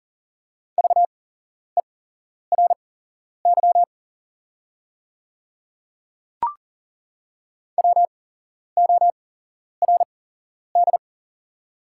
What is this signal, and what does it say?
Morse code, heard as a keyed tone, about 40 wpm, spaced as for 11 wpm, 690 Hz. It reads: VERY WORD
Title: 8x Character Spacing - Top 300 Words - 40wpm